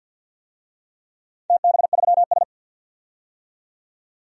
Morse code, heard as a keyed tone, about 47 wpm, 700 Hz. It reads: TB3S
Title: DXCC-100 Call Signs - 4 Characters - 50wpm